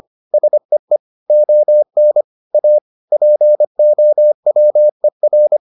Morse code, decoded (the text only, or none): SEE ON A POWER